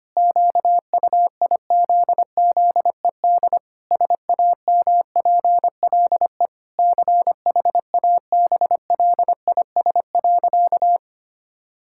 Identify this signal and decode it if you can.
Morse code, heard as a keyed tone, about 25 wpm, 700 Hz. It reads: QUIZZED SAMPLE CHABLIS.
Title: Pangram Challenge v2 - 25wpm